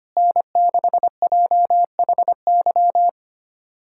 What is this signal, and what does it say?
Morse code, heard as a keyed tone, about 25 wpm, 700 Hz. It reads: N6JHY
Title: USA Callsign DE Callsign - 5 Characters - 25wpm